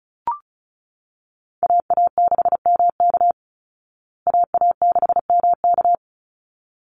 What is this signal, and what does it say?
Morse code, heard as a keyed tone, about 34 wpm, 700 Hz. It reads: AA6MK AA6MK
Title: Call Signs Repeated 3x - 35wpm